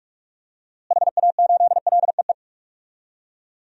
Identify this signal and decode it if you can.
Morse code, heard as a keyed tone, about 43 wpm, 700 Hz. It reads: HA8LEE
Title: DXCC-100 Call Signs - 6 Characters - Repeated 2x - 45wpm